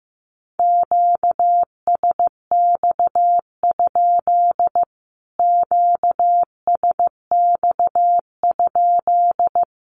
Morse code, heard as a keyed tone, about 15 wpm, 700 Hz. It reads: QSX? QSX?
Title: Q-Codes Repeated 3x 15wpm